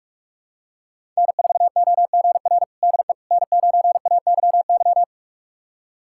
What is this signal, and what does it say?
Morse code, heard as a keyed tone, about 41 wpm, 700 Hz. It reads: N4OGR DE N9AYY
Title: USA Callsign DE Callsign - 5 Characters - 45wpm